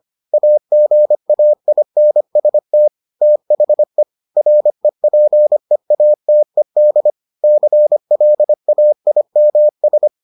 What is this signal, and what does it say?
Morse code, heard as a keyed tone, about 25 wpm, 600 Hz. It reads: AGAINST THE REPEATED CLAIMS